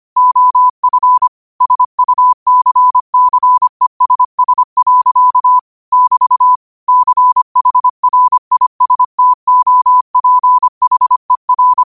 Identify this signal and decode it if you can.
Morse code, heard as a keyed tone, about 25 wpm, 1000 Hz. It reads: OF SUCCESS. <BT> CHRISTOPHER
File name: qotdfiles / 25wpm